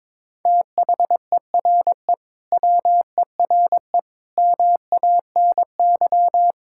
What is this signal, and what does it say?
Morse code, heard as a keyed tone, about 22 wpm, 700 Hz. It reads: THERE WERE MANY
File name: practice / 22wpm / Sentences with 3-Word Phrases - 22wpm